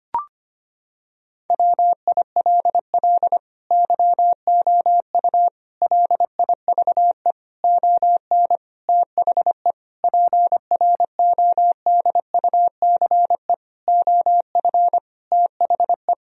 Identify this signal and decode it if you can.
Morse code, heard as a keyed tone, about 25 wpm, 700 Hz. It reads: WILL YOU LIVE ON THE PRODUCE OF THE
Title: Sentences from Top 1000 Words - Review Part 3 - 25wpm